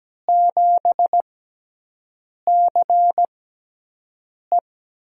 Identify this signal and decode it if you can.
Morse code, heard as a keyed tone, about 17 wpm, spaced as for 6 wpm, 700 Hz. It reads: 7CE
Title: USA Call Signs - 5 Characters - 6x Character Spacing - 17wpm